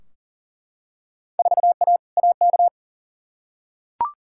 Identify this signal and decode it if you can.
Morse code, heard as a keyed tone, about 40 wpm, 700 Hz. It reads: 4A AK
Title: Field Day Exchanges Encore - 40wpm